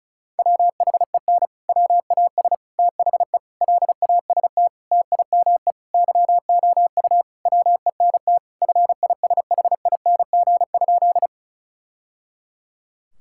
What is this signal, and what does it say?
Morse code, heard as a keyed tone, about 34 wpm, 700 Hz. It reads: WHEN WAS THE LAST TIME YOU WENT FISHING?
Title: Sentences from Top 1000 Words - Review Part 5 - 35wpm